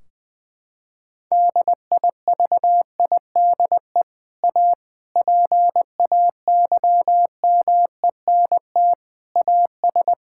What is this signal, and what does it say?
Morse code, heard as a keyed tone, about 20 wpm, 700 Hz. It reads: DIVIDE A PAYMENT AS
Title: Sentences from Top 1600 Words 20wpm